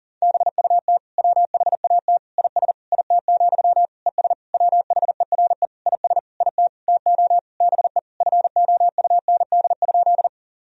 Morse code, heard as a keyed tone, about 40 wpm, 700 Hz. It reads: BUT WHAT IS IT, ES WHERE IS IT TO BE FOUND?
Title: Sentences from Top 300 Words and Abbreviations 40wpm